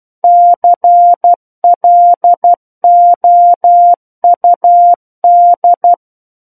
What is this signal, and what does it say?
Morse code, heard as a keyed tone, about 12 wpm, 700 Hz. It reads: CLOUD
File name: Book One / 03-Chapter 3 - 12 WPM